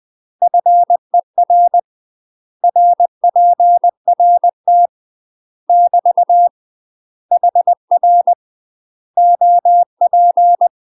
Morse code, heard as a keyed tone, about 20 wpm, 700 Hz. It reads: FER RPRT <BT> HR OP
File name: UK MASTER QSOs TRAINING / 20WPM / UK MASTER QSOs 0044